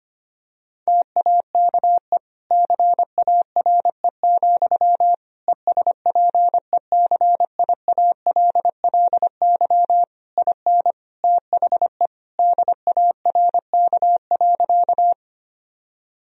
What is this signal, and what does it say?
Morse code, heard as a keyed tone, about 25 wpm, 700 Hz. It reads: TAKE CARE, ESPECIALLY IN THE DARK.
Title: Sentences from Top 1000 Words - Review Part 4 - 25wpm